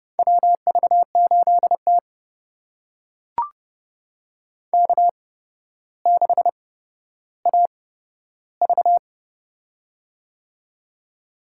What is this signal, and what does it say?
Morse code, heard as a keyed tone, about 32 wpm, spaced as for 13 wpm, 700 Hz. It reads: WV8T K6AV
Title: USA Call Signs - 4 Characters - 8x Character Spacing - 30wpm